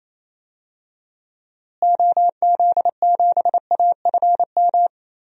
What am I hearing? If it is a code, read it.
Morse code, heard as a keyed tone, about 28 wpm, 700 Hz. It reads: OZ7AFM
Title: DXCC-100 Call Signs - 6 Characters - Repeated 3x - 28wpm